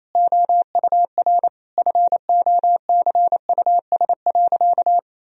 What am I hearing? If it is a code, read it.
Morse code, heard as a keyed tone, about 28 wpm, 700 Hz. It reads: OUR FOCUS.
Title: Sentences from Top 1000 Words - Review Part 4 - 28wpm